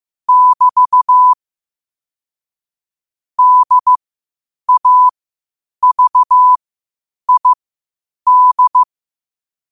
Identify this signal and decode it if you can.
Morse code, heard as a keyed tone, about 15 wpm, spaced as for 8 wpm, 1000 Hz. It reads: <BT> DAVID
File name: qotdfiles / 8wpm